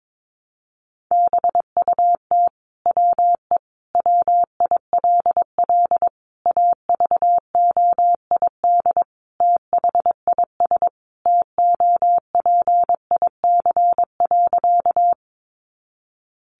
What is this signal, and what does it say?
Morse code, heard as a keyed tone, about 22 wpm, 700 Hz. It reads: BUT WE WILL AVOID THIS TOPIC.
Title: Sentences from Top 1200 Words 22wpm